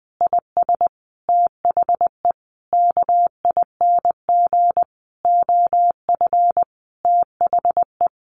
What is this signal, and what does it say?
Morse code, heard as a keyed tone, about 20 wpm, 700 Hz. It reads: IS THE KING OF THE
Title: Sentences from Top 1200 Words 20wpm